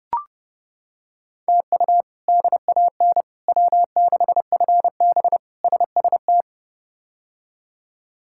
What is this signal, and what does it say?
Morse code, heard as a keyed tone, about 30 wpm, 700 Hz. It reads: TU DAN W6FB SST
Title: K1USN SST QSO Practice - 30wpm